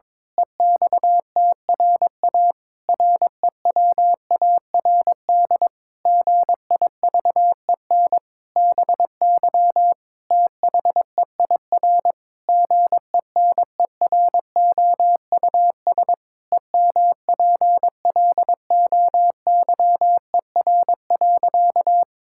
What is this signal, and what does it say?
Morse code, heard as a keyed tone, about 22 wpm, 700 Hz. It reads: EXTRA REWARD GIVEN BY THEIR GENEROUS EMPLOYER.
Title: Pangram Challenge v2 - 22wpm